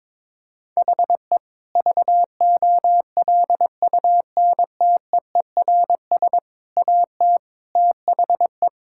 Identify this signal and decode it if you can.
Morse code, heard as a keyed tone, about 22 wpm, 700 Hz. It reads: HE VOLUNTEERS AT THE
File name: Difficult Sentences with Hints - 22wpm